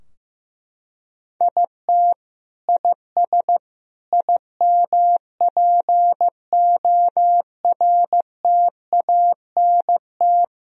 Morse code, heard as a keyed tone, about 15 wpm, 700 Hz. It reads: IT IS IMPORTANT
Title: Sentences from Top 1000 Words - Review Part 4 - 15wpm